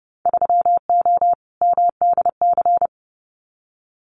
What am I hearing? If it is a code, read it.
Morse code, heard as a keyed tone, about 30 wpm, 700 Hz. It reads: 3O MDC
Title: Winter Field Day Exchanges Encore 30wpm